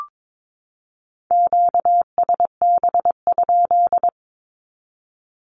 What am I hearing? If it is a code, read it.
Morse code, heard as a keyed tone, about 22 wpm, 700 Hz. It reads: QSB?
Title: Q-Codes 22wpm